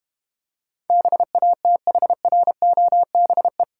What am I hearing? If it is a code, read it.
Morse code, heard as a keyed tone, about 31 wpm, 690 Hz. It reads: BATHROBE
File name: practice / 25wpm / Difficult Words - Speed-Racing with Hints - 25wpm